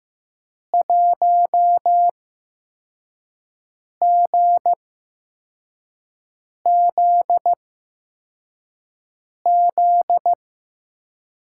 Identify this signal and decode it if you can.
Morse code, heard as a keyed tone, about 15 wpm, spaced as for 4 wpm, 700 Hz. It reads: 1GZZ
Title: USA Call Signs - 5 Characters - 8x Character Spacing - 15wpm